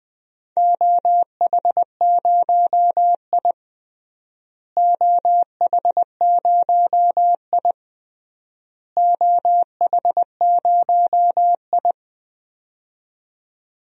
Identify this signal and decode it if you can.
Morse code, heard as a keyed tone, about 20 wpm, 700 Hz. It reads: OH0I OH0I OH0I
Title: DXCC-100 Call Signs - 4 Characters - Repeated 3x - 20wpm